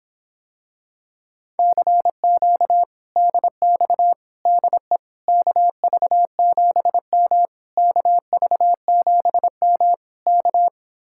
Morse code, heard as a keyed tone, about 26 wpm, 700 Hz. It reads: CQ DX DE KV7M KV7M K